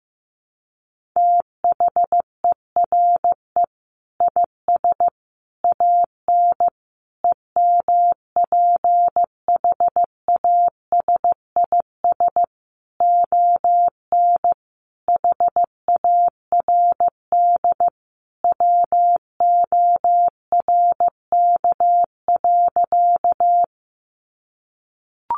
Morse code, heard as a keyed tone, about 15 wpm, 700 Hz. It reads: THERE IS AN EMPHASIS ON HARD WORK.
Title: Sentences from Top 1400 Words 15wpm